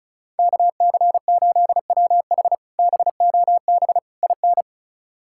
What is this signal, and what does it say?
Morse code, heard as a keyed tone, about 34 wpm, 700 Hz. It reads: KC8WH BOB IN
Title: K1USN SST QSO Practice - 35wpm